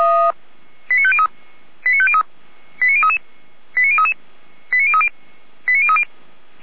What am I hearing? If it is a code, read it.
Five-tone selective call (ZVEI), heard as quick runs of five tones, tones of about 70 ms. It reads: 89503 89503 89033 89033 89030 89030
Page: Server at audio.f85360.feuerwehr-orlamuende.org